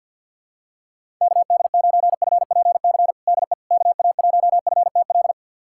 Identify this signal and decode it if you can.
Morse code, heard as a keyed tone, about 53 wpm, 700 Hz. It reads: KD9FPC DE KA1FTL
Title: USA Callsign DE Callsign - 6 Characters - 50wpm